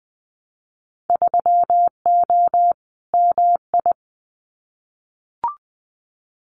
Morse code, heard as a keyed tone, about 20 wpm, 700 Hz. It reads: 3O MI
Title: Winter Field Day Exchanges Encore 20wpm